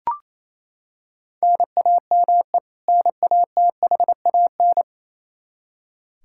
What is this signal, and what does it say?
Morse code, heard as a keyed tone, about 28 wpm, 700 Hz. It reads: NAME NATHAN
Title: Practice QSO - Exchange 1 - Beginning - 28wpm